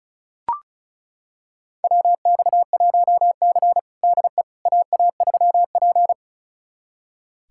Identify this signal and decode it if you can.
Morse code, heard as a keyed tone, about 34 wpm, 700 Hz. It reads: WX1C DE AA3P
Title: USA Callsign DE Callsign - 4 Characters - 35wpm